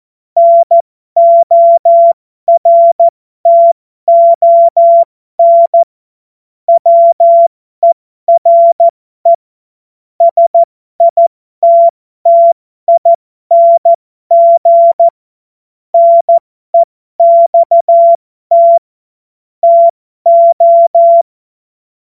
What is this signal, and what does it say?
Morse code, heard as a keyed tone, about 14 wpm, spaced as for 12 wpm, 680 Hz. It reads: NORTON WERE SITTING NEXT TO
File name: Morsekurs / Witze